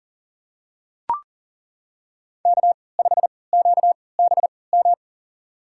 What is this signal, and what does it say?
Morse code, heard as a keyed tone, about 40 wpm, spaced as for 23 wpm, 700 Hz. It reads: K5QBM
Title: USA Call Signs - 5 Characters - 3x Character Spacing - 40wpm